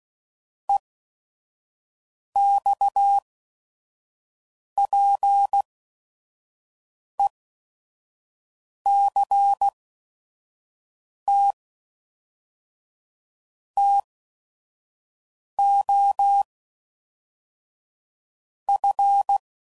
Morse code, heard as a keyed tone, about 16 wpm, spaced as for 5 wpm, 800 Hz. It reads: EXPECT TO F
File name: CW Code Groups @5WPM